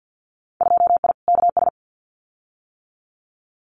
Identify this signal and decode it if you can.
Morse code, heard as a keyed tone, about 50 wpm, 700 Hz. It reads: 2I KS
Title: Winter Field Day Exchanges Encore 50wpm